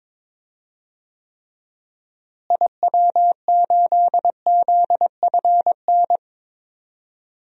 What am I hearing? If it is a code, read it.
Morse code, heard as a keyed tone, about 22 wpm, 700 Hz. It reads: IW8ZFN
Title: DXCC-100 Call Signs - 6 Characters - Repeated 3x - 22wpm